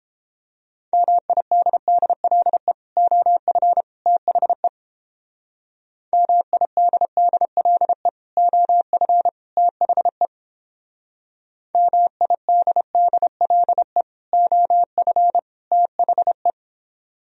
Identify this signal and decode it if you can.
Morse code, heard as a keyed tone, about 29 wpm, 700 Hz. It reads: MIDDLE OF THE MIDDLE OF THE MIDDLE OF THE